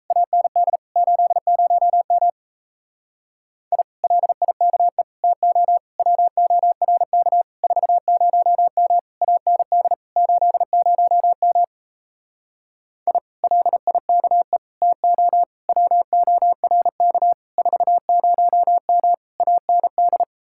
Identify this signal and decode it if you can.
Morse code, heard as a keyed tone, about 37 wpm, 700 Hz. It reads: AND 80M I LIKE TO WORK 40M AND 80M I LIKE TO WORK 40M AND